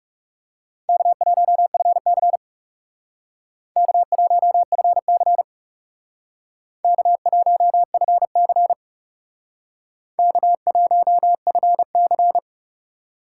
Morse code, heard as a keyed tone, about 38 wpm, 700 Hz. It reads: K1FC K1FC K1FC K1FC